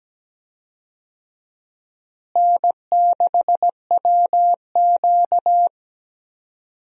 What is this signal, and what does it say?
Morse code, heard as a keyed tone, about 17 wpm, 700 Hz. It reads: N6WQ
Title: USA Call Signs - 4 Characters - Repeated 3x - 17wpm